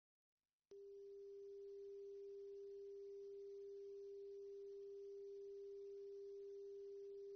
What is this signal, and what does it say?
Dial tone (Japan), 400 Hz.